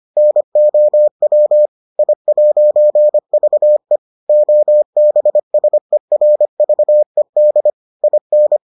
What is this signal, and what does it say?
Morse code, heard as a keyed tone, about 25 wpm, 600 Hz. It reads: NOW I'VE OBSERVED IN